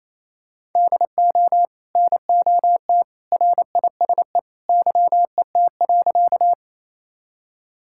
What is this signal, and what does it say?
Morse code, heard as a keyed tone, about 28 wpm, 700 Hz. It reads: DO NOT RISE YET.
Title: Sentences from Top 1000 Words - Review Part 6 - 28wpm